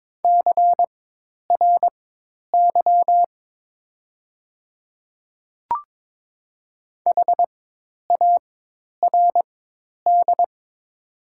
Morse code, heard as a keyed tone, about 22 wpm, spaced as for 10 wpm, 700 Hz. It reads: CRY HARD
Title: 4x Character Spacing - Top 500 Words - 22wpm